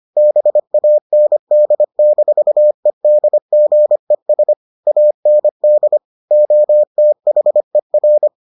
Morse code, heard as a keyed tone, about 25 wpm, 600 Hz. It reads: BAND-EDGES AND OTHER